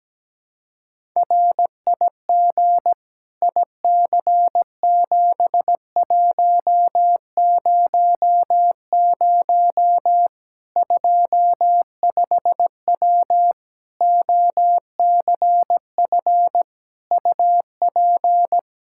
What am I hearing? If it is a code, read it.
Morse code, heard as a keyed tone, about 17 wpm, 700 Hz. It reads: RIG IC7100 25W OCF UP
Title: K1USN SST QSO Practice 17wpm